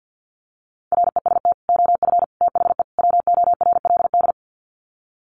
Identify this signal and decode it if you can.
Morse code, heard as a keyed tone, about 50 wpm, 700 Hz. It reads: REST OF THE WORLD
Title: Sets of 4 Words 50wpm